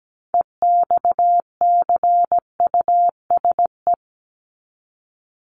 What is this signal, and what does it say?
Morse code, heard as a keyed tone, about 17 wpm, 700 Hz. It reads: EXCUSE
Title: Top 501-750 Words 17wpm